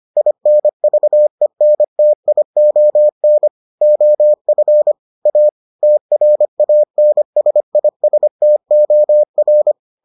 Morse code, heard as a keyed tone, about 25 wpm, 600 Hz. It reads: INVENTION OF A TRANSISTOR